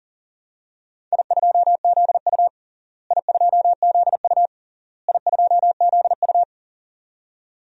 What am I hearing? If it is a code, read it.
Morse code, heard as a keyed tone, about 40 wpm, 700 Hz. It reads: I2ZU I2ZU I2ZU
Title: DXCC-100 Call Signs - 4 Characters - Repeated 3x - 40wpm